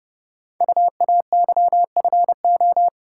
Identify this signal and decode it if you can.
Morse code, heard as a keyed tone, about 30 wpm, 700 Hz. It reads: UAYFO
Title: Code Groups - 5 Letters - 30wpm